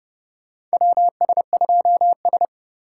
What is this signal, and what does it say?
Morse code, heard as a keyed tone, about 31 wpm, 700 Hz. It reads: WS2S
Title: USA Call Signs - 4x Character Spacing - 30wpm